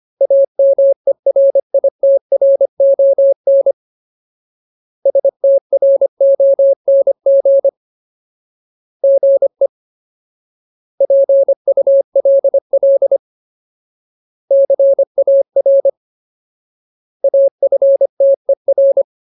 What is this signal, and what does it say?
Morse code, heard as a keyed tone, about 25 wpm, 550 Hz. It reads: AMERITRON STRONG GE PULL CAR AFTER
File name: Ultimate Word List / 25WPM Ultimate